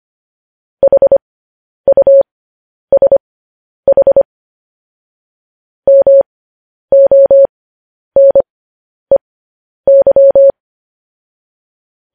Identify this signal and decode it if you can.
Morse code, heard as a keyed tone, about 25 wpm, spaced as for 10 wpm, 550 Hz. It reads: HUSH MONEY